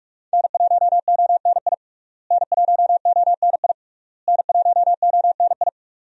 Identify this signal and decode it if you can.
Morse code, heard as a keyed tone, about 44 wpm, 700 Hz. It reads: N1ONI N1ONI N1ONI
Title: USA Call Signs - 5 Characters - Repeated 3x - 45wpm